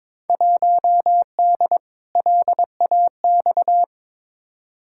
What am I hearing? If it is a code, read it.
Morse code, heard as a keyed tone, about 22 wpm, 700 Hz. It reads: 1D LAX
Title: Field Day Exchanges - 22wpm